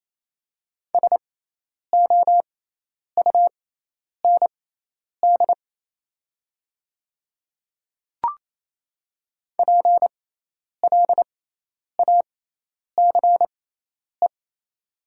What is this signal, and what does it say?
Morse code, heard as a keyed tone, about 28 wpm, spaced as for 10 wpm, 700 Hz. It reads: SOUND PLACE
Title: 6x Character Spacing - Top 200 Words - 28wpm